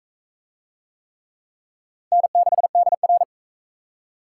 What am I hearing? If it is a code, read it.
Morse code, heard as a keyed tone, about 40 wpm, 700 Hz. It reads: N6DR